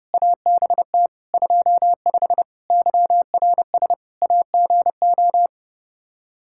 Morse code, heard as a keyed tone, about 30 wpm, 700 Hz. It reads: ABT 25 YRS AGO
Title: Practice QSO - Exchange 3 - More Info - 30wpm